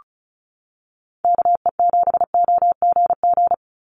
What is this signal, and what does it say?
Morse code, heard as a keyed tone, about 36 wpm, 700 Hz. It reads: KE7OGG
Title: USA Call Signs 35wpm